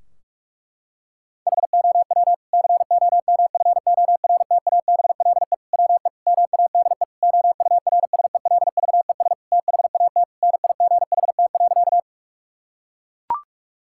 Morse code, heard as a keyed tone, about 42 wpm, 700 Hz. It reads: HOW COMFORTABLE WE MADE OURSELVES THAT NIGHT.
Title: Sentences from Top 1600 Words 45wpm